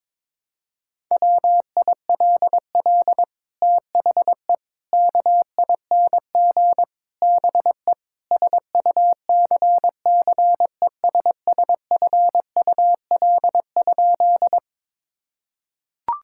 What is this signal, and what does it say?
Morse code, heard as a keyed tone, about 22 wpm, 700 Hz. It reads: WILL THE KING BE SUCCESSFUL?
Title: Sentences from Top 1000 Words - Review Part 6 - 22wpm